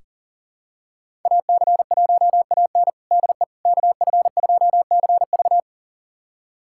Morse code, heard as a keyed tone, about 40 wpm, 690 Hz. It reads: AC1AN DE KF2CV
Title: USA Callsign DE Callsign - 5 Characters - 40wpm